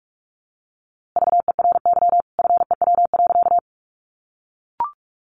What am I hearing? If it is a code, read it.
Morse code, heard as a keyed tone, about 44 wpm, 700 Hz. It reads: VERY FEW.
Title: A Sound Of Thunder 45wpm